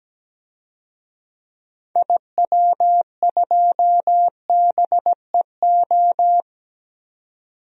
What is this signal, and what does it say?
Morse code, heard as a keyed tone, about 17 wpm, 700 Hz. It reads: IW2BEO
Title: DXCC-100 Call Signs - 6 Characters - 17wpm